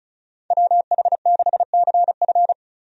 Morse code, heard as a keyed tone, about 36 wpm, 700 Hz. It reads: WH6CF
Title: USA Call Signs - 5 Characters - 35wpm